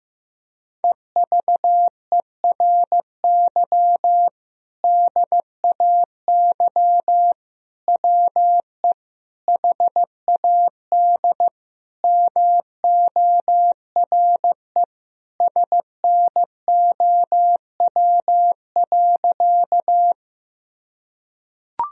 Morse code, heard as a keyed tone, about 15 wpm, 700 Hz. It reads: EVERY DAY WE HAD MORE SNOW.
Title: Sentences from Top 1000 Words - Review Part 3 - 15wpm